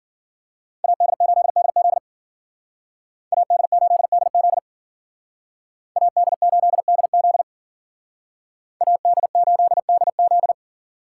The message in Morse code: AD8DZ AD8DZ AD8DZ AD8DZ

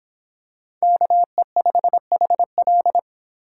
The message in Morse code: KE5HL

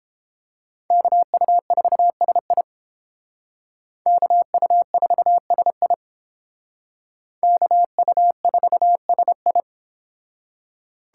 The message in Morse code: KU4SI KU4SI KU4SI

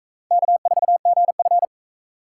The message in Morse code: K4GF

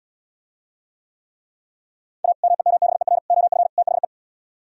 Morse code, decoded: A MEMBER OF THE

